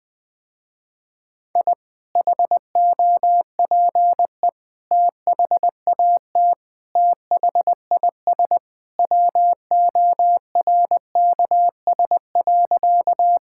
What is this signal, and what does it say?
Morse code, heard as a keyed tone, about 20 wpm, 700 Hz. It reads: I HOPE THAT THIS WORKS.
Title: Sentences with 2-Word Phrases - Spoken Beginning - 20wpm